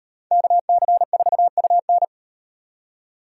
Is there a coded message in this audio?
Yes: Morse code from 0.3 to 2.0 s, heard as a keyed tone, about 38 wpm, 700 Hz. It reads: KC4UN